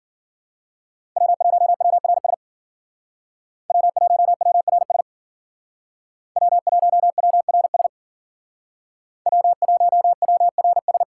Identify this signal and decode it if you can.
Morse code, heard as a keyed tone, about 50 wpm, 700 Hz. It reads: W1WRS W1WRS W1WRS W1WRS